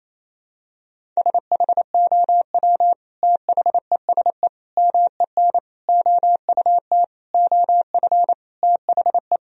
Morse code, heard as a keyed tone, about 28 wpm, 700 Hz. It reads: SHOW THESE MEN OUT OF THE
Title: Sentences from Top 1000 Words - Review Part 1 - 28wpm